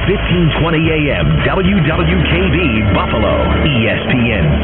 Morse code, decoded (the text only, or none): none